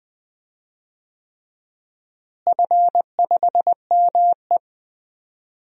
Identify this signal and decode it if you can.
Morse code, heard as a keyed tone, about 20 wpm, 700 Hz. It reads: F5ME
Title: DXCC-100 Call Signs - 4 Characters - Repeated 3x - 20wpm